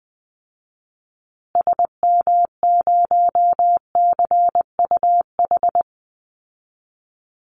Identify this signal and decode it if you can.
Morse code, heard as a keyed tone, about 20 wpm, 700 Hz. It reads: SM0CUH
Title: Call Signs Repeated 2x - 20wpm